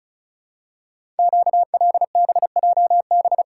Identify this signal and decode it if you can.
Morse code, heard as a keyed tone, about 36 wpm, 700 Hz. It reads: QLBJB
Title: Code Groups - 5 Letters - 35wpm